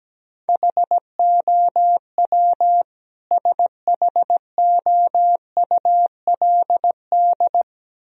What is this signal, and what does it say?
Morse code, heard as a keyed tone, about 17 wpm, 700 Hz. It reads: HOW SHOULD
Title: Sentences from Top 1000 Words - Review Part 2 - 17wpm